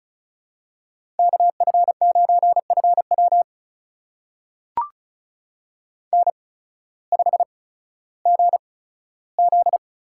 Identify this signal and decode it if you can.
Morse code, heard as a keyed tone, about 34 wpm, spaced as for 16 wpm, 700 Hz. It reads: KF9FW N5GZ
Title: USA Call Signs - 5 Characters - 8x Character Spacing - 35wpm